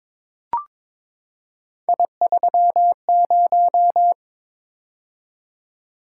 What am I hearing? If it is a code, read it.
Morse code, heard as a keyed tone, about 22 wpm, 700 Hz. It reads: I30
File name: practice / 22wpm / Three Letter-Number Combinations - 22wpm